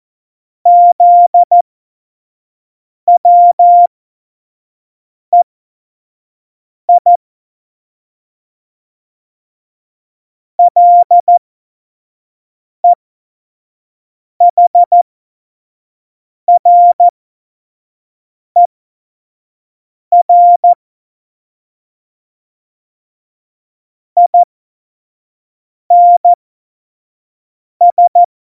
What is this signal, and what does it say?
Morse code, heard as a keyed tone, about 14 wpm, spaced as for 5 wpm, 700 Hz. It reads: ZWEI LEHRER INS